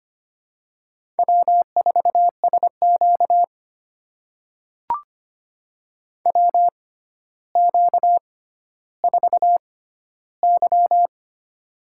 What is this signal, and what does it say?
Morse code, heard as a keyed tone, about 25 wpm, spaced as for 13 wpm, 700 Hz. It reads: W4SQ WQ4Y
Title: USA Call Signs - 4 Characters - 6x Character Spacing - 25wpm